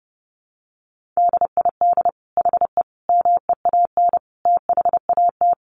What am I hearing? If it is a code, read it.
Morse code, heard as a keyed tone, about 28 wpm, 700 Hz. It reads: DID HE MEAN THAT
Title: Sentences from Top 1400 Words 30wpm